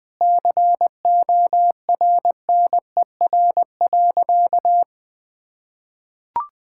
Morse code, heard as a keyed tone, about 20 wpm, 700 Hz. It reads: CORNER.
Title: Sentences from Top 1600 Words 20wpm